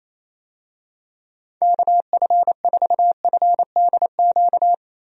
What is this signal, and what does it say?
Morse code, heard as a keyed tone, about 28 wpm, 700 Hz. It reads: KF4FDQ
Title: USA Call Signs - 6 Characters - Repeated 3x - 28wpm